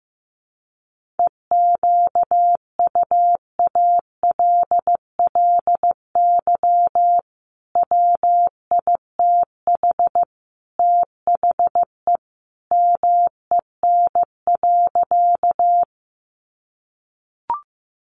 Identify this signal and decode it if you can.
Morse code, heard as a keyed tone, about 15 wpm, 700 Hz. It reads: EQUALLY WITH THE MEN.
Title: Sentences from Top 1400 Words 15wpm